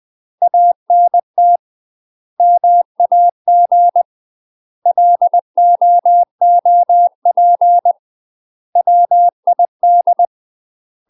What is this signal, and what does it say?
Morse code, heard as a keyed tone, about 20 wpm, 700 Hz. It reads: ANT MAG LOOP WID
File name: UK MASTER QSOs TRAINING / 20WPM / UK MASTER QSOs 0070